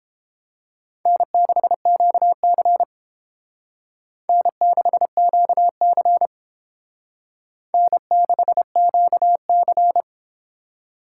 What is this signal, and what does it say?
Morse code, heard as a keyed tone, about 31 wpm, 700 Hz. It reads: N6QC N6QC N6QC